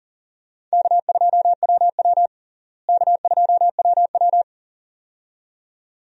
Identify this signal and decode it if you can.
Morse code, heard as a keyed tone, about 40 wpm, 700 Hz. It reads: K2WW K2WW
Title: USA Call Signs - 4 Characters - Repeated 2x - 40wpm